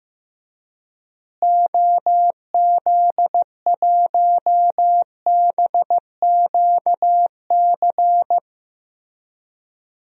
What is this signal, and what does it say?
Morse code, heard as a keyed tone, about 15 wpm, 700 Hz. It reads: OZ1BQC
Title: DXCC-100 Call Signs - 6 Characters - Repeated 2x - 15wpm